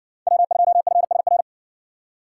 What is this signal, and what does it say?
Morse code, heard as a keyed tone, about 55 wpm, 700 Hz. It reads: W2FSR